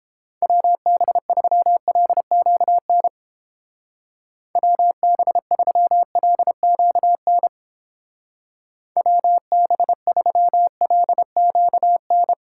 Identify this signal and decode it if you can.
Morse code, heard as a keyed tone, about 29 wpm, 700 Hz. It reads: WB3LQN WB3LQN WB3LQN